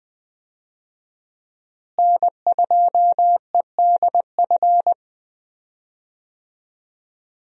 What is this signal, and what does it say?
Morse code, heard as a keyed tone, about 20 wpm, 700 Hz. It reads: N2EDF
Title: USA Call Signs - 5 Characters - Repeated 3x - 20wpm